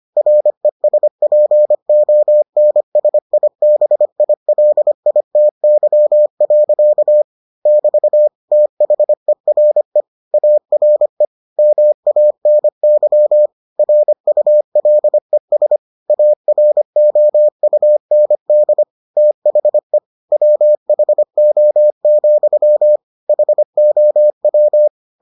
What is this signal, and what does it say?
Morse code, heard as a keyed tone, about 25 wpm, 600 Hz. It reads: RESPONSIBILITY. <BT> THERE ARE MANY RULES AROUND THE WHO, HOW